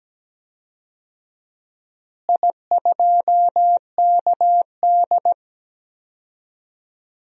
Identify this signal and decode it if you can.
Morse code, heard as a keyed tone, about 17 wpm, 700 Hz. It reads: I2KD
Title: DXCC-100 Call Signs - 4 Characters - Repeated 2x - 17wpm